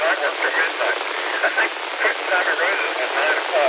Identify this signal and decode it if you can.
Emergency Alert System (SAME) tones, heard as harsh buzzing bursts, with speech. It is heard from 0.3 to 2.8 s.